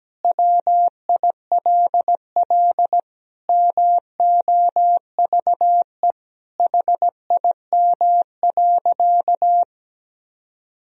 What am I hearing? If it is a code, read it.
Morse code, heard as a keyed tone, about 17 wpm, 700 Hz. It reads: WILL MOVE HIM.
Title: Sentences from Top 1000 Words - Review Part 1 - 17wpm